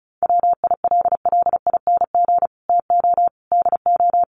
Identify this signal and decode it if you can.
Morse code, heard as a keyed tone, about 34 wpm, 700 Hz. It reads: WILLING TO DO
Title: Sentences from Top 1400 Words 35wpm